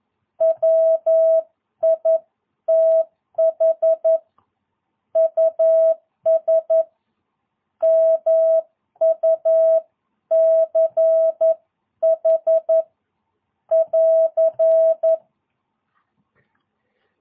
Morse code, heard as a keyed tone, about 11 wpm, 650 Hz. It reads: WITH US MUCH <AR>
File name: jh8cbh / 2022cw